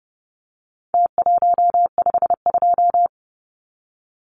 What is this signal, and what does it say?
Morse code, heard as a keyed tone, about 30 wpm, 700 Hz. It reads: T152
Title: Numbers - Contest Style - 4-Digit 30wpm